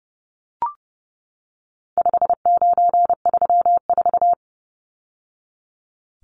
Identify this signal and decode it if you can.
Morse code, heard as a keyed tone, about 30 wpm, 700 Hz. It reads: 5934
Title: Numbers - 4-Digit 30wpm